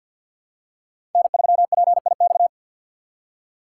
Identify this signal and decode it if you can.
Morse code, heard as a keyed tone, about 50 wpm, 700 Hz. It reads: N3PIX
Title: USA Call Signs - 5 Characters - 2x Character Spacing - 50wpm